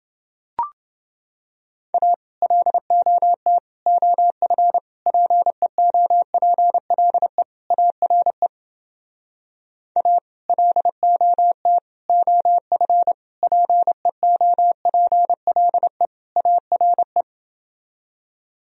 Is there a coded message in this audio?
Yes: Morse code from 1.9 to 17.2 s, heard as a keyed tone, about 31 wpm, 700 Hz. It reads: A LOT OF PEOPLE ARE A LOT OF PEOPLE ARE